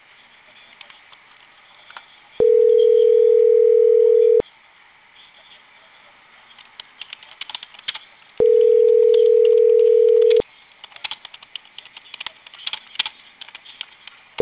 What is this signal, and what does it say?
Ringback tone (North America), 440 + 480 Hz.